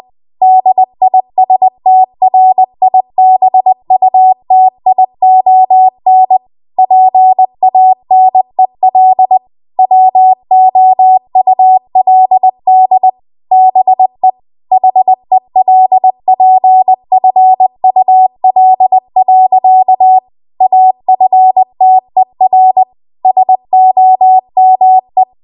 Morse code, heard as a keyed tone, about 20 wpm, 750 Hz. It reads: DISTRIBUTION PANEL WOULD BE HELPFUL. AFTER SOME
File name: Archive / 20 WPM / 20WPM